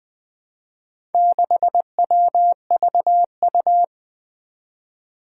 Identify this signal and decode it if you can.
Morse code, heard as a keyed tone, about 20 wpm, 700 Hz. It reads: 6WVU
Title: Four Challenging Letter-Number Combinations 20wpm